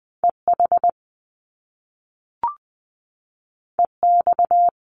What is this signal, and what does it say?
Morse code, heard as a keyed tone, about 20 wpm, 700 Hz. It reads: EH EX
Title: Top Two Letter Combinations in Words 20wpm